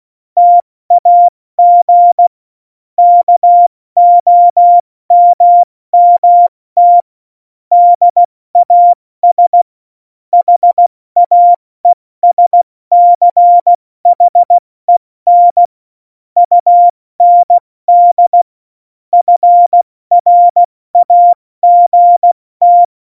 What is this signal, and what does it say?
Morse code, heard as a keyed tone, about 16 wpm, 700 Hz. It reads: TAG KOMMT DAS HAESCHEN UND FRAGT
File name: Morsekurs / Witze / Witz 14 April 2020, 70 F80